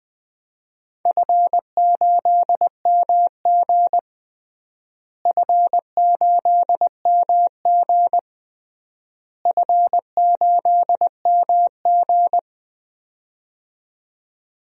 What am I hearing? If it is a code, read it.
Morse code, heard as a keyed tone, about 20 wpm, 700 Hz. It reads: F8MG F8MG F8MG
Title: DXCC-100 Call Signs - 4 Characters - Repeated 3x - 20wpm